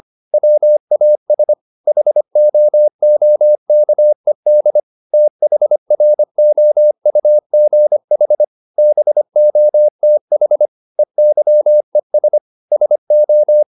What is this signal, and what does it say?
Morse code, heard as a keyed tone, about 25 wpm, 600 Hz. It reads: WAS HOOKED THROUGH BOTH EYES SO
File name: the old man and the sea part 1 25wpm